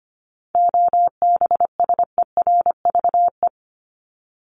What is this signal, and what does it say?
Morse code, heard as a keyed tone, about 25 wpm, 700 Hz. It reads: OBSERVE
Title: ED Suffix - Top 100 Words with Base Words 25wpm